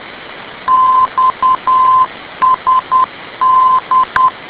Morse code, heard as a keyed tone, about 10 wpm, 1030 Hz. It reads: XSD
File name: info / xsd ndb